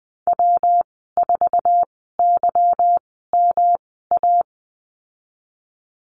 Morse code, heard as a keyed tone, about 19 wpm, spaced as for 14 wpm, 700 Hz. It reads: W4YMA
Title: 154 Lesson 25g - Letter p - Call Signs 20wpm